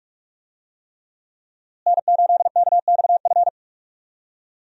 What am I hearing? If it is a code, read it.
Morse code, heard as a keyed tone, about 42 wpm, 700 Hz. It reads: N8KXF
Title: USA Call Signs - 5 Characters - Repeated 3x - 45wpm